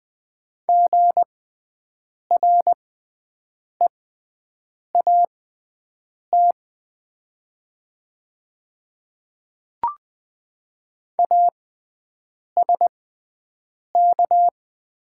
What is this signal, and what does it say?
Morse code, heard as a keyed tone, about 20 wpm, spaced as for 7 wpm, 700 Hz. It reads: GREAT ASK
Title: 6x Character Spacing - Top 200 Words - 20wpm